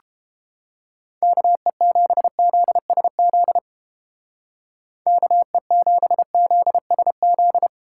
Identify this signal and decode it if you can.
Morse code, heard as a keyed tone, about 30 wpm, 700 Hz. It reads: KE7ZSZ KE7ZSZ